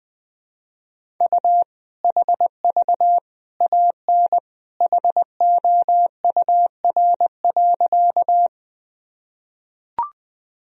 Morse code, heard as a keyed tone, about 20 wpm, 700 Hz. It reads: U HV AN HOUR.
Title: Sentences from Top 400 Words and Abbreviations 20wpm